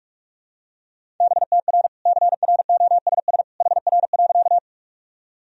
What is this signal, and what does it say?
Morse code, heard as a keyed tone, about 46 wpm, 700 Hz. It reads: BTR CROSS HR.
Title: Sentences from Top 400 Words and Abbreviations 45wpm